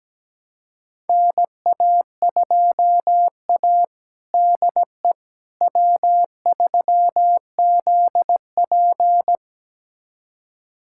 Morse code, heard as a keyed tone, about 17 wpm, 700 Hz. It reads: NA2A DE W3ZP
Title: USA Callsign DE Callsign - 4 Characters - 17wpm